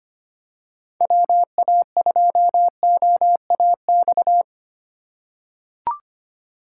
Morse code, heard as a keyed tone, about 25 wpm, 700 Hz. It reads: WA2OAX
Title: USA Call Signs - 6 Characters - 3x Character Spacing - 25wpm